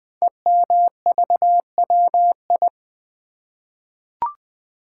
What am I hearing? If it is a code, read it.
Morse code, heard as a keyed tone, about 20 wpm, 700 Hz. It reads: EMVWI
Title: Code Groups - 5 Letters - 20wpm